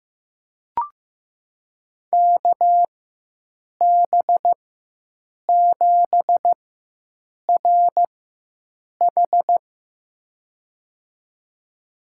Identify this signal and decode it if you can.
Morse code, heard as a keyed tone, about 15 wpm, spaced as for 7 wpm, 700 Hz. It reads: KB7RH
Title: USA Call Signs - 5 Characters - 4x Character Spacing - 15wpm